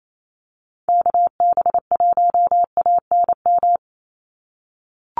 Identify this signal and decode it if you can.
Morse code, heard as a keyed tone, about 28 wpm, 700 Hz. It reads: KB1ANM
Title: USA Call Signs 28wpm